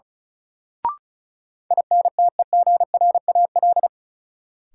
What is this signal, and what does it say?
Morse code, heard as a keyed tone, about 36 wpm, 700 Hz. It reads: INTEGRAL
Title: Sets of 5 Words - New Word Drill - 35wpm